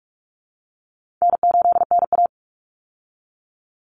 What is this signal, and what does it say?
Morse code, heard as a keyed tone, about 44 wpm, 700 Hz. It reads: N8NA
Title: Call Signs Repeated 2x - 45wpm